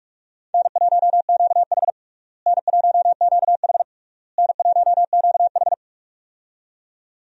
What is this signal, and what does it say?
Morse code, heard as a keyed tone, about 42 wpm, 700 Hz. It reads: N1QH N1QH N1QH
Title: USA Call Signs - 4 Characters - Repeated 3x - 45wpm